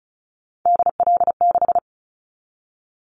Morse code, heard as a keyed tone, about 34 wpm, 700 Hz. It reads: DL6
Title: Callsign Prefixes for Top 50 DXCC - 35wpm